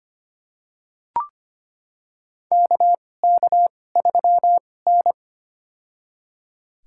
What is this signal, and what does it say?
Morse code, heard as a keyed tone, about 25 wpm, spaced as for 18 wpm, 700 Hz. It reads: KK3N
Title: USA Call Signs - 4 Characters - 2x Character Spacing - 25wpm